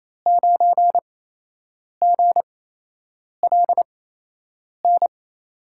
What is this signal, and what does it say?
Morse code, heard as a keyed tone, about 28 wpm, spaced as for 8 wpm, 700 Hz. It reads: 9GLN
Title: USA Call Signs - 5 Characters - 8x Character Spacing - 28wpm